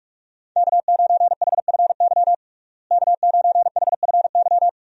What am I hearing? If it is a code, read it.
Morse code, heard as a keyed tone, about 46 wpm, 700 Hz. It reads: K9HFY K9HFY
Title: USA Call Signs - 5 Characters - Repeated 2x - 45wpm